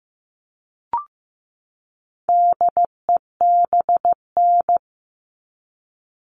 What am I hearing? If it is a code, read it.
Morse code, heard as a keyed tone, about 15 wpm, 700 Hz. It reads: DEBN
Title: All Cut Numbers - 4-Digit 15wpm